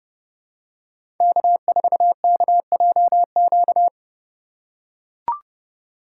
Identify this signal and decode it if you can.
Morse code, heard as a keyed tone, about 30 wpm, 700 Hz. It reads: K4KJQ
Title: USA Call Signs - 5 Characters - 3x Character Spacing - 30wpm